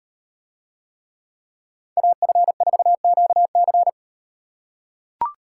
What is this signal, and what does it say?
Morse code, heard as a keyed tone, about 37 wpm, 700 Hz. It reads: AF4QC